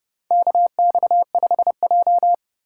Morse code, heard as a keyed tone, about 32 wpm, 700 Hz. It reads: KX5J
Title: USA Call Signs - 3x Character Spacing - 30wpm